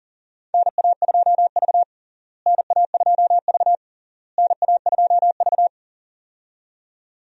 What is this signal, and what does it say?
Morse code, heard as a keyed tone, about 40 wpm, 700 Hz. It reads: NA2V NA2V NA2V
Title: USA Call Signs - 4 Characters - Repeated 3x - 40wpm